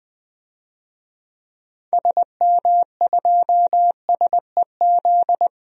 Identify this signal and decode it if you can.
Morse code, heard as a keyed tone, about 20 wpm, 700 Hz. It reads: SM2SEZ
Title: DXCC-100 Call Signs - 6 Characters - 20wpm